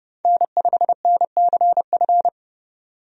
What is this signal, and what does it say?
Morse code, heard as a keyed tone, about 32 wpm, 700 Hz. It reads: N5NCF
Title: USA Call Signs - 5 Characters - 30wpm